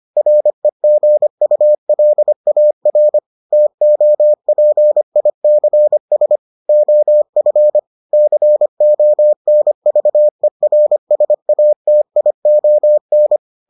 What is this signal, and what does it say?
Morse code, heard as a keyed tone, about 25 wpm, 600 Hz. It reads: REGULAR TOPICS OF CONVERSATION